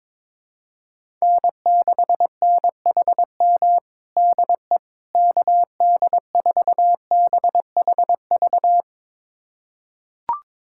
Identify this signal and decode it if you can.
Morse code, heard as a keyed tone, about 22 wpm, 700 Hz. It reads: N6NHM DE KD4BHV
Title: USA Callsign DE Callsign 22wpm